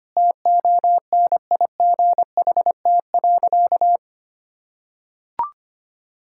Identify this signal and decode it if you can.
Morse code, heard as a keyed tone, about 25 wpm, 700 Hz. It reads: TONIGHT.
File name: practice / 25wpm / Sentences with 2-Word Phrases - 25wpm